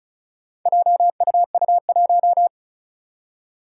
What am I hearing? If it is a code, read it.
Morse code, heard as a keyed tone, about 36 wpm, 690 Hz. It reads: JUU1
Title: Four Challenging Letter-Number Combinations 35wpm